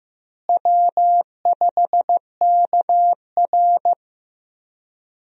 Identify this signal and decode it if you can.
Morse code, heard as a keyed tone, about 15 wpm, 700 Hz. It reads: W5KR
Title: USA Call Signs - 4 Characters - Repeated 2x - 15wpm